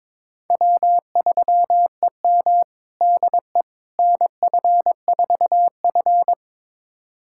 Morse code, heard as a keyed tone, about 22 wpm, 700 Hz. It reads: W3EM DE NF4F
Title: USA Callsign DE Callsign - 4 Characters - 22wpm